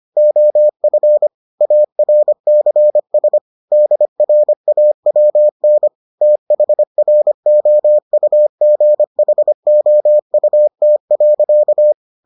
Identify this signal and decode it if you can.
Morse code, heard as a keyed tone, about 25 wpm, 600 Hz. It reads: OF ARCS DRAWN THROUGHOUT.